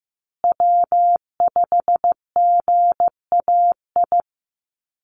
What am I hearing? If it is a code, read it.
Morse code, heard as a keyed tone, about 15 wpm, 700 Hz. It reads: W5GAI
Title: Call Signs Repeated 2x - 15wpm